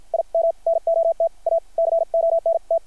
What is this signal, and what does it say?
Morse code, heard as a keyed tone, about 50 wpm, spaced as for 43 wpm, 660 Hz. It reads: IM NOT A CONT
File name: archived / CRSnet